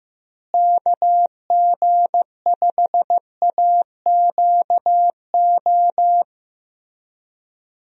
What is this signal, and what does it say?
Morse code, heard as a keyed tone, about 15 wpm, 700 Hz. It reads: KG5AQO
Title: USA Call Signs - 6 Characters - Repeated 3x - 15wpm